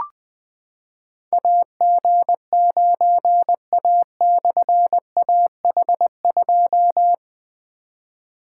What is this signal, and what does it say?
Morse code, heard as a keyed tone, about 20 wpm, 700 Hz. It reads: AG9A/AH2
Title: Difficult Call Signs 20wpm